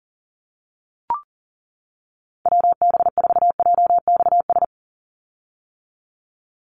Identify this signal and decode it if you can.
Morse code, heard as a keyed tone, about 41 wpm, 700 Hz. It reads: WB4JXS
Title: USA Call Signs 40wpm